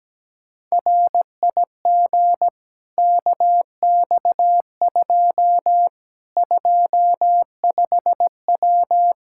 Morse code, heard as a keyed tone, about 17 wpm, 700 Hz. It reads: RIG KX2 25W
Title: K1USN SST QSO Practice 17wpm